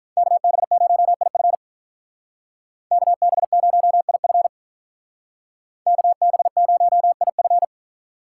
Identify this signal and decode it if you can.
Morse code, heard as a keyed tone, about 43 wpm, 700 Hz. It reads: KB0IF KB0IF KB0IF
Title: Speed Racing - USA 2x2 Call Signs 35wpm